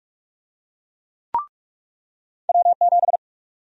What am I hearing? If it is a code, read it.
Morse code, heard as a keyed tone, about 46 wpm, 700 Hz. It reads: W7
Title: USA Call Sign Prefixes - 45wpm